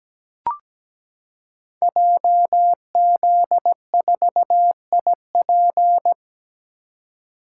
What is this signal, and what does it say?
Morse code, heard as a keyed tone, about 17 wpm, 700 Hz. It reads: JZ4IP
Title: Code Groups - 5 Letter:Numbers - 17wpm